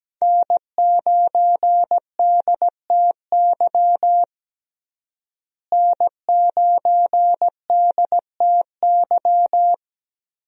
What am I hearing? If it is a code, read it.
Morse code, heard as a keyed tone, about 17 wpm, 700 Hz. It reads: N9DTY N9DTY
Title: USA Call Signs - 5 Characters - Repeated 3x - 17wpm